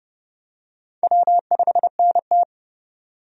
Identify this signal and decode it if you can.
Morse code, heard as a keyed tone, about 30 wpm, 700 Hz. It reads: W5NT
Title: USA Call Signs - 4 Characters - 30wpm